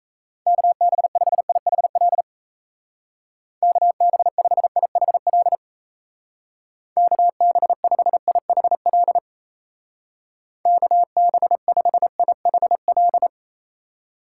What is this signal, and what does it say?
Morse code, heard as a keyed tone, about 37 wpm, 700 Hz. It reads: KB5IHL KB5IHL KB5IHL KB5IHL